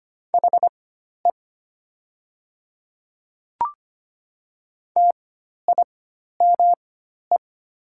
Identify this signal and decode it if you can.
Morse code, heard as a keyed tone, about 25 wpm, spaced as for 12 wpm, 690 Hz. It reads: HE TIME
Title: 4x Character Spacing - Top 100 Words - 25wpm